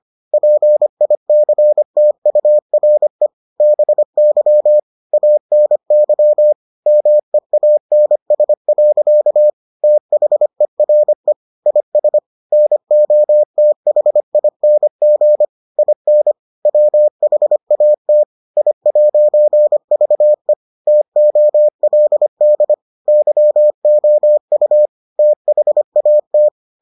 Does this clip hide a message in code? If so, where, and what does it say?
Yes: Morse code from 0.3 to 26.5 s, heard as a keyed tone, about 25 wpm, 600 Hz. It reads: PICTURE BY ANY MEANS. THERE IS NOTHING IN WHAT I'VE TOLD YOU THAT